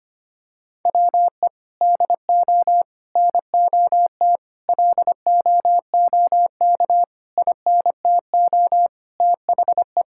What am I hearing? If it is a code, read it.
Morse code, heard as a keyed tone, about 25 wpm, 700 Hz. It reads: WE DO NOT LOOK INTO THE
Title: Sentences from Top 1600 Words 25wpm